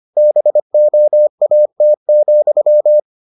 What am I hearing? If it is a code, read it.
Morse code, heard as a keyed tone, about 25 wpm, 600 Hz. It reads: BOAT,